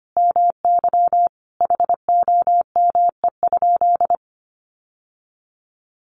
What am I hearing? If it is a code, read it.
Morse code, heard as a keyed tone, about 25 wpm, 700 Hz. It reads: MY HOME?
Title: Sentences from Top 1200 Words 25wpm